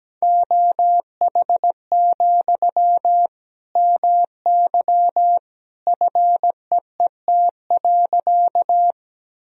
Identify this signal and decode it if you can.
Morse code, heard as a keyed tone, about 17 wpm, 700 Hz. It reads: OH, MY FEET.
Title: Sentences from Top 1000 Words - Review Part 1 - 17wpm